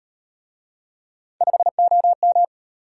Morse code, heard as a keyed tone, about 36 wpm, 700 Hz. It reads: 5OM